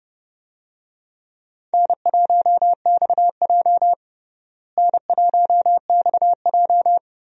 Morse code, heard as a keyed tone, about 30 wpm, 700 Hz. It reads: N1XJ N1XJ
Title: USA Call Signs - 4 Characters - Repeated 2x - 30wpm